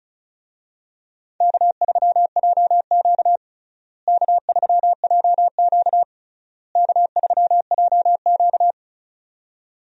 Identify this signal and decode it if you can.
Morse code, heard as a keyed tone, about 34 wpm, 700 Hz. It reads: K3JQ K3JQ K3JQ
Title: USA Call Signs - 4 Characters - Repeated 3x - 35wpm